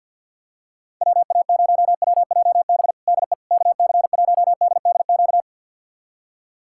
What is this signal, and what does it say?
Morse code, heard as a keyed tone, about 48 wpm, 700 Hz. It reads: WA0WJB DE KC1DDQ
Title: USA Callsign DE Callsign - 6 Characters - 50wpm